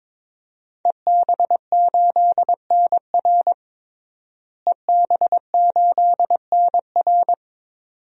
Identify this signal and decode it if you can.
Morse code, heard as a keyed tone, about 22 wpm, 700 Hz. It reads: EB8NR EB8NR
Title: DXCC-100 Call Signs - 5 Characters - Repeated 2x - 22wpm